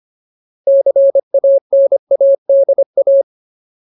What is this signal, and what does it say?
Morse code, heard as a keyed tone, about 25 wpm, 550 Hz. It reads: CANADA
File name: Ultimate Word List / 25WPM Ultimate